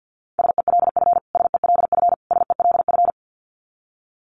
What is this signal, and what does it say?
Morse code, heard as a keyed tone, about 50 wpm, 700 Hz. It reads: SELF SELF SELF
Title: Top 500 Words Repeated 50wpm